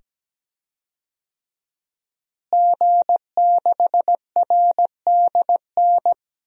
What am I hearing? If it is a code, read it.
Morse code, heard as a keyed tone, about 17 wpm, 700 Hz. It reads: G6RDN
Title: DXCC-100 Call Signs - 5 Characters - Repeated 2x - 17wpm